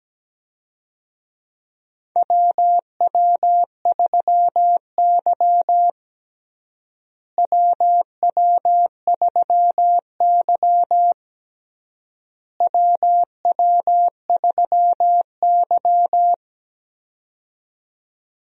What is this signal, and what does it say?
Morse code, heard as a keyed tone, about 17 wpm, 700 Hz. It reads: WW3Y WW3Y WW3Y
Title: USA Call Signs - 4 Characters - Repeated 3x - 17wpm